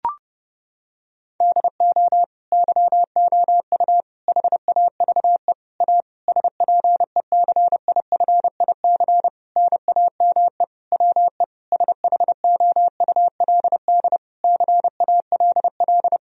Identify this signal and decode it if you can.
Morse code, heard as a keyed tone, about 30 wpm, 700 Hz. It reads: DO YOU HAVE A SPECIFIC NAME WE SHOULD CALL
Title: Sentences from Top 1000 Words - Review Part 3 - 30wpm